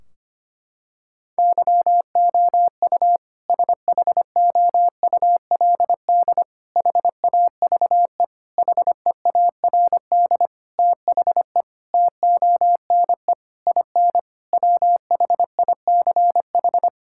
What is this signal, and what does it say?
Morse code, heard as a keyed tone, about 25 wpm, 700 Hz. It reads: YOU SHOULD HAVE HEARD THE TONE IN WHICH
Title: Sentences from Top 1600 Words 25wpm